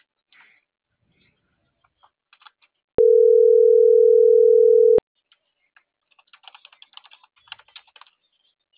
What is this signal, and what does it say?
Ringback tone (North America), 440 + 480 Hz.